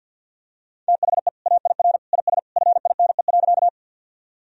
Morse code, heard as a keyed tone, about 52 wpm, 700 Hz. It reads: THE AIR IS FINE.
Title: Sentences from Top 1000 Words - Review Part 1 - 50wpm